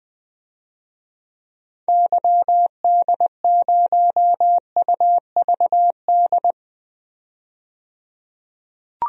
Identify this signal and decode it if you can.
Morse code, heard as a keyed tone, about 20 wpm, 700 Hz. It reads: YD0UVD
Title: DXCC-100 Call Signs - 6 Characters - 20wpm